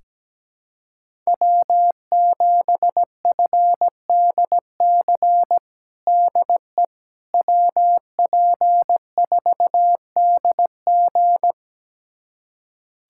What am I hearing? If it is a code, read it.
Morse code, heard as a keyed tone, about 17 wpm, 700 Hz. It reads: W7FDC DE WP4DG
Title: USA Callsign DE Callsign - 5 Characters - 17wpm